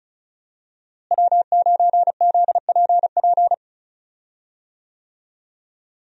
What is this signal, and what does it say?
Morse code, heard as a keyed tone, about 34 wpm, 700 Hz. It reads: W9ZPP
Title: USA Call Signs - 5 Characters - Repeated 3x - 35wpm